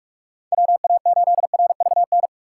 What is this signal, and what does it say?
Morse code, heard as a keyed tone, about 46 wpm, 700 Hz. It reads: WA8RVN
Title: USA Call Signs - 6 Characters - 2x Character Spacing - 45wpm